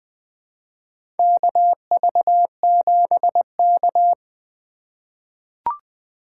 Morse code, heard as a keyed tone, about 20 wpm, 700 Hz. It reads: KV7K
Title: USA Call Signs - 4 Characters - 3x Character Spacing - 20wpm